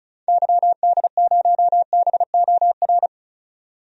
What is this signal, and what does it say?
Morse code, heard as a keyed tone, about 34 wpm, 700 Hz. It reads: YD0BOR
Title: DXCC-100 Call Signs - 6 Characters - 35wpm